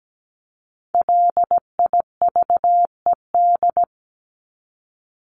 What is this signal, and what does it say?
Morse code, heard as a keyed tone, about 17 wpm, 700 Hz. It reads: LIVED
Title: ED Suffix - Top 100 Words with Base Words 17wpm